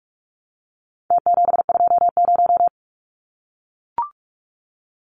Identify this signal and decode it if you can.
Morse code, heard as a keyed tone, about 44 wpm, 700 Hz. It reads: T720
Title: Numbers - Contest Style - 4-Digit 45wpm